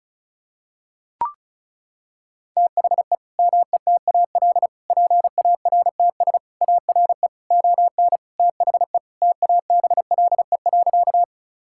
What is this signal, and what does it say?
Morse code, heard as a keyed tone, about 36 wpm, 700 Hz. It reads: THE METAL PARTS ARE ON THE TABLE.
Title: Sentences from Top 1000 Words - Review Part 6 - 35wpm